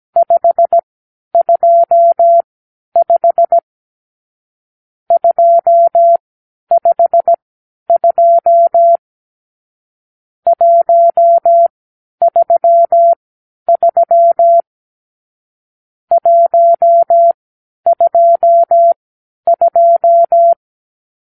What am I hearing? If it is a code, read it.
Morse code, heard as a keyed tone, about 17 wpm, spaced as for 10 wpm, 680 Hz. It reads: 525 252 133 122
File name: arrl / Review (1-5)